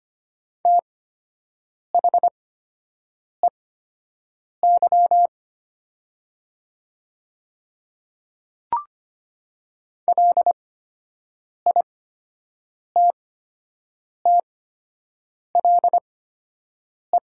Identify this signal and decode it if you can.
Morse code, heard as a keyed tone, about 25 wpm, spaced as for 7 wpm, 700 Hz. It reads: THEY LITTLE
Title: 8x Character Spacing - Top 500 Words - 25wpm